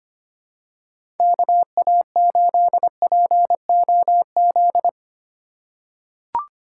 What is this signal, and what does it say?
Morse code, heard as a keyed tone, about 25 wpm, 700 Hz. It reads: KA8POZ
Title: K1USN SST QSO Practice - 25wpm